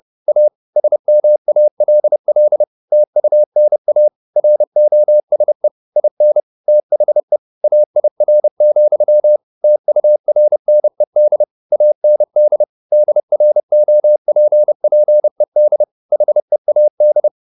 Morse code, heard as a keyed tone, about 30 wpm, 600 Hz. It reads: A SMALL TUNA ROSE IN THE AIR, TURNED AND DROPPED HEAD
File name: the old man and the sea / the old man and the sea 1 30wpm